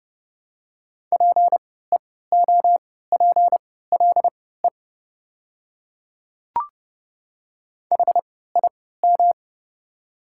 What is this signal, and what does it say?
Morse code, heard as a keyed tone, about 30 wpm, spaced as for 17 wpm, 700 Hz. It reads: PEOPLE HIM
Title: 3x Character Spacing - Top 200 Words - 30wpm